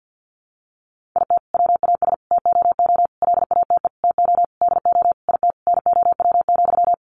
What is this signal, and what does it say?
Morse code, heard as a keyed tone, about 50 wpm, 700 Hz. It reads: IT WAS TOO LATE TO DO IT NOW,
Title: Sentences from Top 600 Words 50wpm